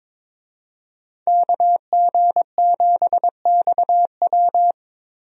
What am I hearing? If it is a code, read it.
Morse code, heard as a keyed tone, about 22 wpm, 700 Hz. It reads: KG7XW
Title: USA Call Signs - 5 Characters - 22wpm